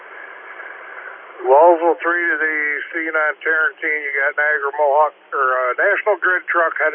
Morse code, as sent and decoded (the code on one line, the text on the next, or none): none